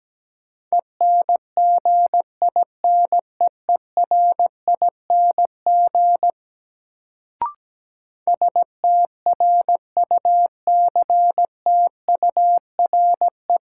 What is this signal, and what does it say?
Morse code, heard as a keyed tone, about 17 wpm, 700 Hz. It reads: ENGINEERING STRUCTURE
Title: Sentences from Top 1500 Words - New Word Drill Alt - 17wpm